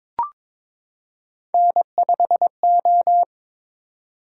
Morse code, heard as a keyed tone, about 22 wpm, 700 Hz. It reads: N5O
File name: practice / 22wpm / Three Letter-Number Combinations - 22wpm